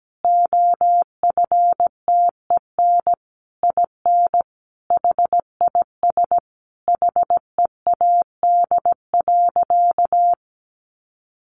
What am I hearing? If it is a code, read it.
Morse code, heard as a keyed tone, about 17 wpm, 700 Hz. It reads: OFTEN IN HIS HEAD.
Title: Sentences from Top 1400 Words 17wpm